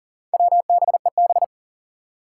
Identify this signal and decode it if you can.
Morse code, heard as a keyed tone, about 40 wpm, 700 Hz. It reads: WBEB
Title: Four Challenging Letter-Number Combinations 40wpm